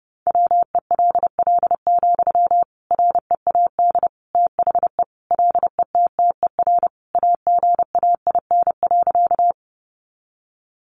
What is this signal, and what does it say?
Morse code, heard as a keyed tone, about 30 wpm, 700 Hz. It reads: WELL, READ THE LETTER AGAIN.
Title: Sentences from Top 300 Words 30wpm